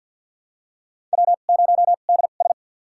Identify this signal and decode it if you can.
Morse code, heard as a keyed tone, about 50 wpm, spaced as for 36 wpm, 700 Hz. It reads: W0DS
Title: USA Call Signs - 2x Character Spacing - 50wpm